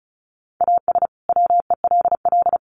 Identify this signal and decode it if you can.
Morse code, heard as a keyed tone, about 34 wpm, 700 Hz. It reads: AS WELL
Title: Sets of 2 Words 35wpm